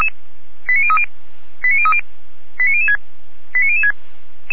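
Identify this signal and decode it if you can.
Five-tone selective call (ZVEI), heard as quick runs of five tones, tones of about 80 ms. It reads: 89030 89030 89006 89006
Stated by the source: ZVEI five-tone selective call 89504, received with an SDR.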